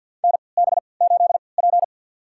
Morse code, heard as a keyed tone, about 47 wpm, spaced as for 28 wpm, 700 Hz. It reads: NB8P
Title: USA Call Signs - 4 Characters - 3x Character Spacing - 50wpm